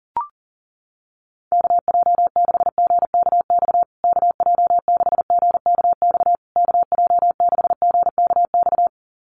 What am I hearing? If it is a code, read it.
Morse code, heard as a keyed tone, about 40 wpm, 700 Hz. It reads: KJ6GKX KJ6GKX KJ6GKX
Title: USA Call Signs Repeated 3x 40wpm